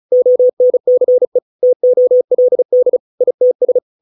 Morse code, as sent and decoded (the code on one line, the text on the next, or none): --- -. -.-. / - --- .-.. -.. / .. - ...
ONCE TOLD ITS